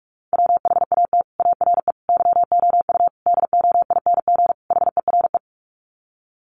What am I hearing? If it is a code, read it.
Morse code, heard as a keyed tone, about 44 wpm, 700 Hz. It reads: WHAT ARE YOU DOING HERE
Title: Sets of 5 Words 45wpm